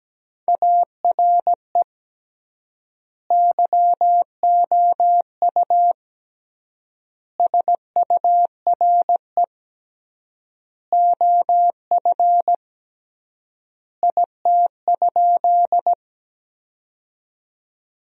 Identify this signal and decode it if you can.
Morse code, heard as a keyed tone, about 17 wpm, 700 Hz. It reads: ARE YOU SURE OF IT?
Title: 3x Word-Spacing - Sentences from Top 300 Words - 17wpm